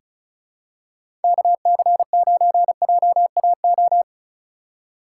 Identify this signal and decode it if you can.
Morse code, heard as a keyed tone, about 36 wpm, 700 Hz. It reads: KC9JAO
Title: USA Call Signs - 6 Characters - 3x Character Spacing - 35wpm